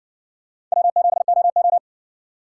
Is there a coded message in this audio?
Yes: Morse code from 0.7 to 1.8 s, heard as a keyed tone, about 57 wpm, 700 Hz. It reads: W7OO